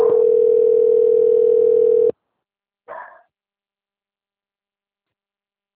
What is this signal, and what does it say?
Ringback tone (North America), 440 + 480 Hz.
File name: out- - - - - .36844